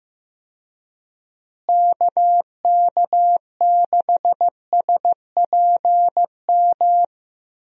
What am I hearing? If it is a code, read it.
Morse code, heard as a keyed tone, about 15 wpm, 700 Hz. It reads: KK6SPM
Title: USA Call Signs - 6 Characters - Repeated 2x - 15wpm